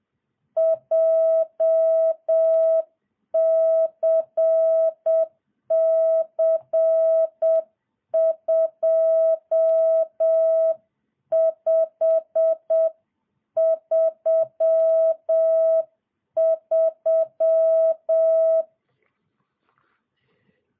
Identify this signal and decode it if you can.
Morse code, heard as a keyed tone, about 7 wpm, 660 Hz. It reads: JCC2533